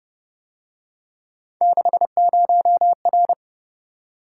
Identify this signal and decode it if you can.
Morse code, heard as a keyed tone, about 28 wpm, 700 Hz. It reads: 60R
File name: Speed-Racing - Three Letter-Number Combinations - 25wpm